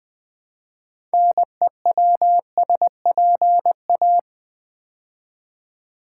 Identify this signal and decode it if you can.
Morse code, heard as a keyed tone, about 20 wpm, 700 Hz. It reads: NEWSPA
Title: Build A Word - 3 Letters At A Time - 9-Letter Words - 20wpm